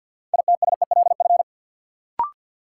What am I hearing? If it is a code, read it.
Morse code, heard as a keyed tone, about 52 wpm, 700 Hz. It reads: ITSELF
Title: Sentences from Top 800 Words - New Word Drill - 50wpm